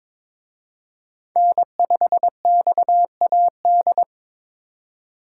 Morse code, heard as a keyed tone, about 26 wpm, 700 Hz. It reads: N5XAD
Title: Speed Racing - USA 1x3 Call Signs 22wpm